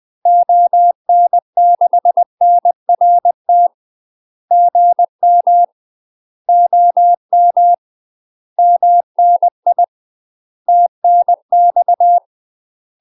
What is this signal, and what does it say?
Morse code, heard as a keyed tone, about 20 wpm, 700 Hz. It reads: ON6NRT GM OM MNI TNX
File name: UK MASTER QSOs TRAINING / 20WPM / UK MASTER QSOs 0018